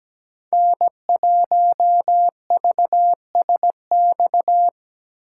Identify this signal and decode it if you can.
Morse code, heard as a keyed tone, about 17 wpm, 700 Hz. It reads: N1VSX
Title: USA Call Signs - 5 Characters - 3x Character Spacing - 17wpm